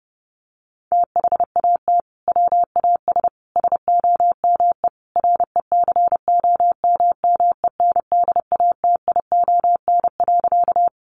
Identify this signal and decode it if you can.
Morse code, heard as a keyed tone, about 28 wpm, 700 Hz. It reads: THAT WAS SOME RECOMMENDATION.
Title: Sentences from Top 1400 Words 30wpm